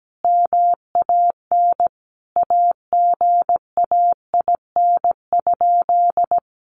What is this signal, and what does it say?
Morse code, heard as a keyed tone, about 17 wpm, 700 Hz. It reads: MAN AGAIN?
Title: Sentences from Top 1400 Words 17wpm